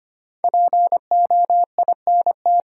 Morse code, heard as a keyed tone, about 25 wpm, 700 Hz. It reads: POINT
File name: 5 Letter Words - First 3 Letters Given - 25wpm